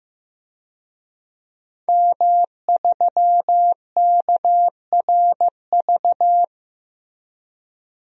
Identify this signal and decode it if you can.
Morse code, heard as a keyed tone, about 15 wpm, 700 Hz. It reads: M3KRV
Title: DXCC-100 Call Signs - 5 Characters - Repeated 2x - 15wpm